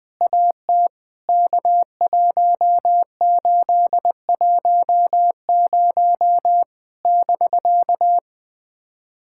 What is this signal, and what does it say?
Morse code, heard as a keyed tone, about 20 wpm, 700 Hz. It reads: AT K1810 <BK>
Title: POTA QSO Practice - 20wpm